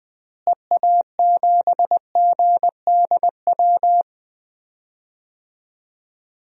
Morse code, heard as a keyed tone, about 20 wpm, 700 Hz. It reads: EA7GDW
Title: DXCC-100 Call Signs - 6 Characters - Repeated 2x - 20wpm